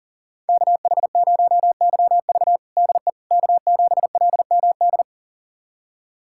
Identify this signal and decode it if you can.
Morse code, heard as a keyed tone, about 40 wpm, 700 Hz. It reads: KH0YV DE K7LMD
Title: USA Callsign DE Callsign - 5 Characters - 40wpm